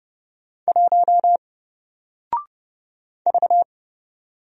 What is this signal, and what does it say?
Morse code, heard as a keyed tone, about 30 wpm, 700 Hz. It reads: 1 V
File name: Single-Letter-Number-30wpm